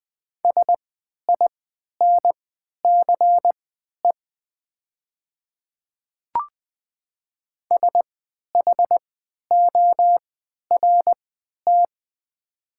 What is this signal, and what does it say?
Morse code, heard as a keyed tone, about 20 wpm, spaced as for 11 wpm, 700 Hz. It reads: SINCE SHORT
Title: 3x Character Spacing - Top 500 Words - 20wpm